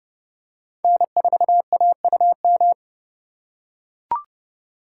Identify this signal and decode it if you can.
Morse code, heard as a keyed tone, about 30 wpm, 700 Hz. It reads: N4AUM
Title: USA Call Signs - 5 Characters - 3x Character Spacing - 30wpm